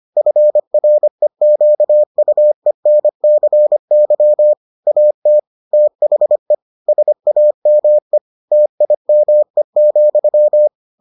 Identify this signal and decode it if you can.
Morse code, heard as a keyed tone, about 25 wpm, 600 Hz. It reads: FREQUENCY AT THE SAME TIME,